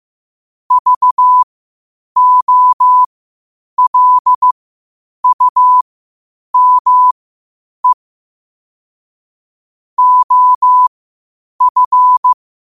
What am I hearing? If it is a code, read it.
Morse code, heard as a keyed tone, about 15 wpm, spaced as for 8 wpm, 1000 Hz. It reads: VOLUME OF
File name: qotdfiles / 8wpm